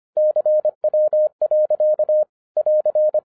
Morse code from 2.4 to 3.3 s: <AR>